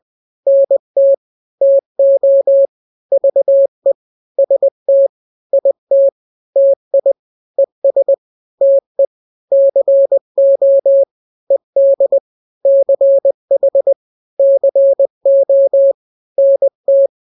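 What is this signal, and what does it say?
Morse code, heard as a keyed tone, about 20 wpm, 550 Hz. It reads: NT TO VE ST IT TI ES TE CO ED CH CO NT
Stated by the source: Morse drill of two-letter n-grams at 20/18 WPM (Farnsworth), from KK7RR.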